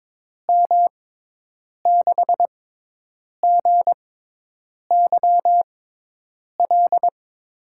Morse code, heard as a keyed tone, about 22 wpm, spaced as for 8 wpm, 700 Hz. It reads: M6GYL
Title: USA Call Signs - 6 Characters - 6x Character Spacing - 22wpm